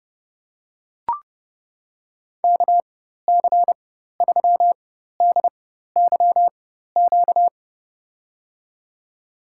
Morse code, heard as a keyed tone, about 31 wpm, spaced as for 14 wpm, 700 Hz. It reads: KC3DYQ
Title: USA Call Signs - 4x Character Spacing - 30wpm